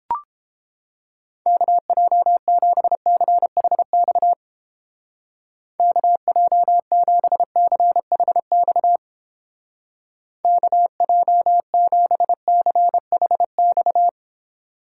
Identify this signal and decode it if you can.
Morse code, heard as a keyed tone, about 29 wpm, 700 Hz. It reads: KJ7CHX KJ7CHX KJ7CHX